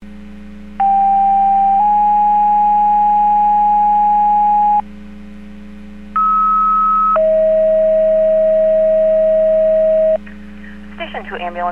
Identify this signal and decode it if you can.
Two-tone page, heard as a one-second tone, then a three-second tone, about 790 then 830 Hz, about 1290 then 640 Hz, with speech. It is heard from 0.8 to 10.2 s.